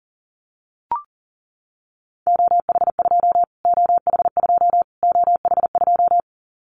Morse code, heard as a keyed tone, about 40 wpm, 700 Hz. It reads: OH2 OH2 OH2
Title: Callsign Prefixes for Top 25 DXCC - Repeated 3x - 40wpm